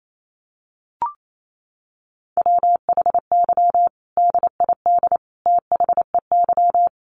Morse code, heard as a keyed tone, about 28 wpm, 700 Hz. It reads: WHY DID THEY
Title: Sentences from Top 1500 Words 28wpm